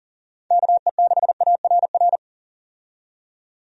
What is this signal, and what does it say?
Morse code, heard as a keyed tone, about 39 wpm, 700 Hz. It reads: KE6ARR
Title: USA Call Signs - 6 Characters - 40wpm